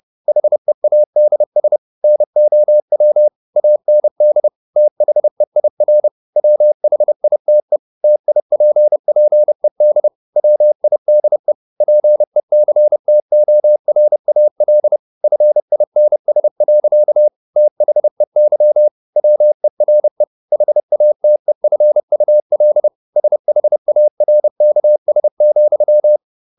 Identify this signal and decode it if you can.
Morse code, heard as a keyed tone, about 30 wpm, 600 Hz. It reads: HEADS NOW AND THEIR WHITE TIPPED WIDE PECTORAL FINS. THEY WERE HATEFUL SHARKS,